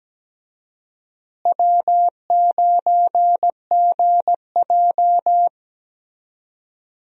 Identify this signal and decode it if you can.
Morse code, heard as a keyed tone, about 17 wpm, 700 Hz. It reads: W9GJ
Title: USA Call Signs - 4 Characters - Repeated 3x - 17wpm